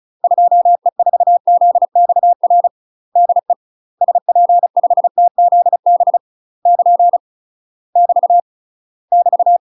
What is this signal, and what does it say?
Morse code, heard as a keyed tone, about 34 wpm, 700 Hz. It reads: 2E4ZXR DE SP5TZB <KN> <BT> <BT>
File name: UK MASTER QSOs TRAINING / 35WPM / UK MASTER QSOs 0001